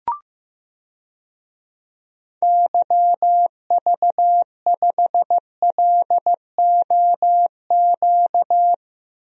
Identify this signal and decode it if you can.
Morse code, heard as a keyed tone, about 15 wpm, 700 Hz. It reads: YV5LOQ
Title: DXCC-100 Call Signs - 6 Characters - Repeated 2x - 15wpm